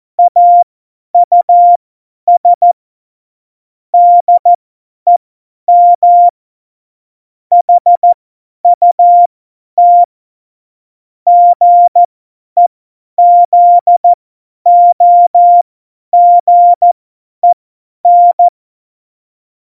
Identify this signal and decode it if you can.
Morse code, heard as a keyed tone, about 14 wpm, spaced as for 10 wpm, 700 Hz. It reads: AUS DEM HUT GEZOGEN
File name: Morsekurs / Witze / Witz 01. Mai 2018, 50 F70